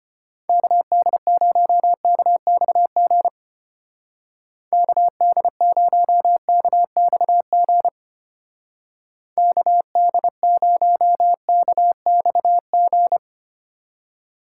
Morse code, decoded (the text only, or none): KD0KXG KD0KXG KD0KXG